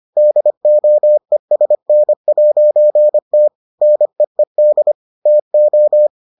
Morse code, heard as a keyed tone, about 25 wpm, 600 Hz. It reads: DOESN'T NEED TO